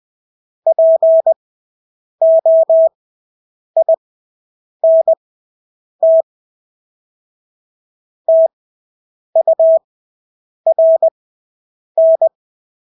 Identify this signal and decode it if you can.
Morse code, heard as a keyed tone, about 20 wpm, spaced as for 8 wpm, 650 Hz. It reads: POINT TURN
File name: Engelske ord 20/8 wpm